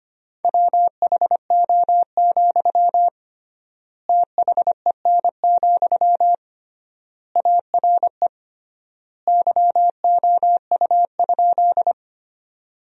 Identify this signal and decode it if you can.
Morse code, heard as a keyed tone, about 25 wpm, 700 Hz. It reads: WHO, THEN, ARE YOU?
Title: 3x Word-Spacing - Sentences from Top 100 Words - 25wpm